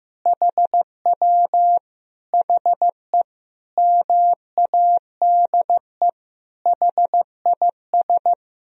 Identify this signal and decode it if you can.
Morse code, heard as a keyed tone, about 15 wpm, 700 Hz. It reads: HW HE MADE HIS
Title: Sentences from Top 300 Words and Abbreviations 15wpm